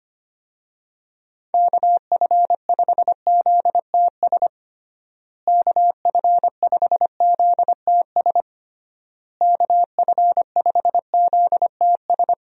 Morse code, heard as a keyed tone, about 25 wpm, 700 Hz. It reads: KF5ZTS KF5ZTS KF5ZTS
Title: USA Call Signs - 6 Characters - Repeated 3x - 25wpm